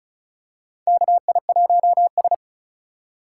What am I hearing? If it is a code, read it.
Morse code, heard as a keyed tone, about 36 wpm, 700 Hz. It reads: KI1S
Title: USA Call Signs - 4 Characters - 8x Character Spacing - 35wpm